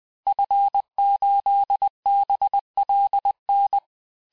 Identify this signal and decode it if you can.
Morse code, heard as a keyed tone, about 20 wpm, 800 Hz. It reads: F8BLN
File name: cours-du-lundi files / cours-1584-Fin-de-QSO-vitesse-19mpm